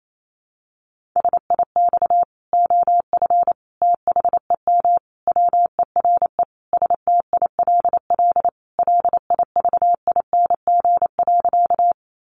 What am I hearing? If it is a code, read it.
Morse code, heard as a keyed tone, about 28 wpm, 700 Hz. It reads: SIX OF THEM WERE STILL LIVING.
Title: Sentences from Top 1100 Words 28wpm